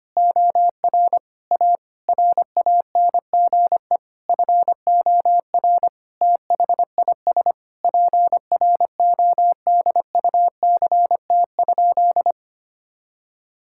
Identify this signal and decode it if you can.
Morse code, heard as a keyed tone, about 25 wpm, 700 Hz. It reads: OR A RANGE FOR THIS PRODUCT?
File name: practice / 25wpm / Sentences with 9-Word Phrases - 25wpm